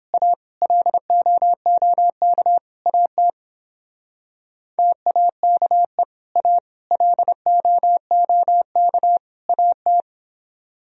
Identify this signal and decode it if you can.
Morse code, heard as a keyed tone, about 31 wpm, 700 Hz. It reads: A LOOK AT TAKE A LOOK AT